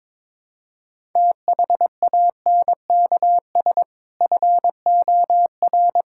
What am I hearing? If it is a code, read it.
Morse code, heard as a keyed tone, about 22 wpm, 700 Hz. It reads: THANKS FOR
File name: practice / 22wpm / Build A Phrase - 5-Word Phrases - 22wpm